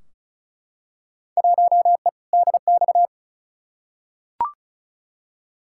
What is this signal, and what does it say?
Morse code, heard as a keyed tone, about 36 wpm, 700 Hz. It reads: 1E DX
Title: Field Day Exchanges Encore - 35wpm